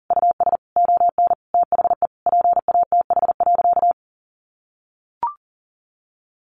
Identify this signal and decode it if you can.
Morse code, heard as a keyed tone, about 40 wpm, 700 Hz. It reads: US ON THE PATH.
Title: Sentences from Top 1400 Words 40wpm